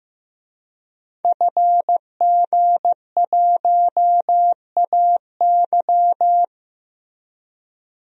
Morse code, heard as a keyed tone, about 16 wpm, 700 Hz. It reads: FG1AY
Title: DXCC-100 Call Signs - 5 Characters - Repeated 3x - 15wpm